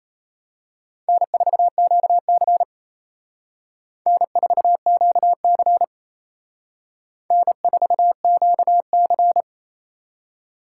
Morse code, heard as a keyed tone, about 32 wpm, 700 Hz. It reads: N4QC N4QC N4QC
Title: Speed Racing - USA 1x2 Call Signs 28wpm